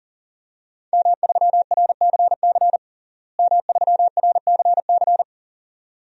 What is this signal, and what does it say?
Morse code, heard as a keyed tone, about 40 wpm, 700 Hz. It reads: M3RCC M3RCC
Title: DXCC-100 Call Signs - 5 Characters - Repeated 2x - 40wpm